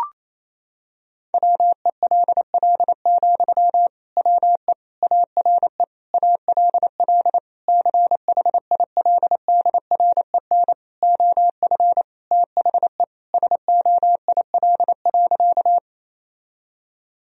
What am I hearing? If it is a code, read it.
Morse code, heard as a keyed tone, about 27 wpm, 700 Hz. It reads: WELL, WE ARE ALL CHILDREN OF THE SOIL.
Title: Sentences from Top 1000 Words - Review Part 4 - 28wpm